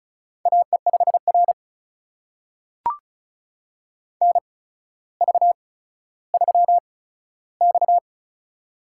Morse code, heard as a keyed tone, about 36 wpm, spaced as for 15 wpm, 700 Hz. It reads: AE5R NV3X
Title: USA Call Signs - 4 Characters - 8x Character Spacing - 35wpm